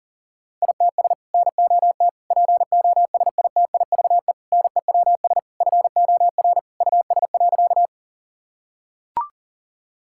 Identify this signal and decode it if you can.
Morse code, heard as a keyed tone, about 40 wpm, 700 Hz. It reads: ITS NOT POSITIVE NEWS FOR US.
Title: Sentences from Top 1000 Words - Review Part 5 - 40wpm